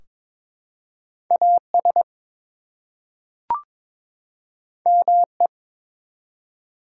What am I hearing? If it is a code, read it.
Morse code, heard as a keyed tone, about 22 wpm, 700 Hz. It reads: AS ME
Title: Two Letter Words - 22wpm